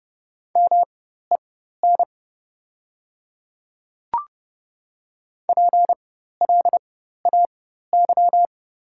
Morse code, heard as a keyed tone, about 30 wpm, spaced as for 14 wpm, 700 Hz. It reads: MEN PLAY
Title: 4x Character Spacing - Top 200 Words - 30wpm